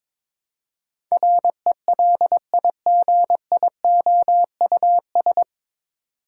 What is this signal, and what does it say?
Morse code, heard as a keyed tone, about 22 wpm, 700 Hz. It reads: RELIGIOUS
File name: 9 Letter Words - First 3 Letters Given - 22wpm